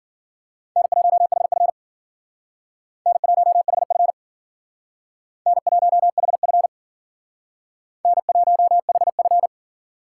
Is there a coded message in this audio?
Yes: Morse code from 0.8 to 9.5 s, heard as a keyed tone, about 53 wpm, 700 Hz. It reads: N1HF N1HF N1HF N1HF